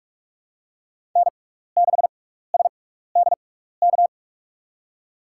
Morse code, heard as a keyed tone, about 46 wpm, spaced as for 15 wpm, 700 Hz. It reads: N6SDK
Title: USA Call Signs - 6x Character Spacing - 45wpm